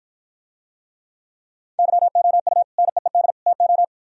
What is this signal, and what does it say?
Morse code, heard as a keyed tone, about 59 wpm, 700 Hz. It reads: YOU NEED TO